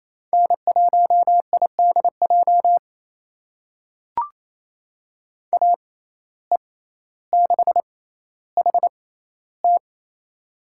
Morse code, heard as a keyed tone, about 27 wpm, spaced as for 14 wpm, 700 Hz. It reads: N1IDJ AE6HT
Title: USA Call Signs - 5 Characters - 6x Character Spacing - 28wpm